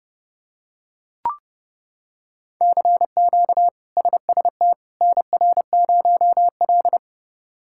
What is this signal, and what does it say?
Morse code, heard as a keyed tone, about 32 wpm, 700 Hz. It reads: CQ SST NR0L
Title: K1USN SST QSO Practice - 30wpm